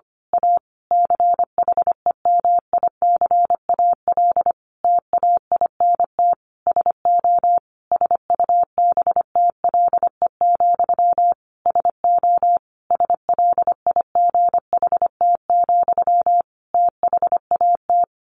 Morse code, heard as a keyed tone, about 25 wpm, 700 Hz. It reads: A CHEMICAL TAINT SO SUBTLE, SO SLIGHT, THAT
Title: A Sound Of Thunder 25wpm